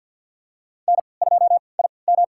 Morse code, heard as a keyed tone, about 52 wpm, spaced as for 29 wpm, 700 Hz. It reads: N2IM
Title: USA Call Signs - 3x Character Spacing - 50wpm